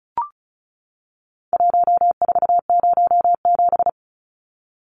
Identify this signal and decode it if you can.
Morse code, heard as a keyed tone, about 34 wpm, 700 Hz. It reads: 1407
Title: Numbers - Contest Style - 4-Digit 35wpm